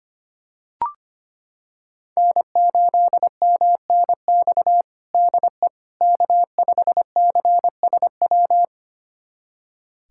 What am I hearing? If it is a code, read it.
Morse code, heard as a keyed tone, about 25 wpm, 700 Hz. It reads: N8MNX DE K5CSW
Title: USA Callsign DE Callsign 25wpm